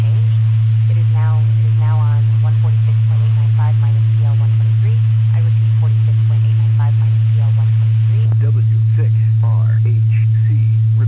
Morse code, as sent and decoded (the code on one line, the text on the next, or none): none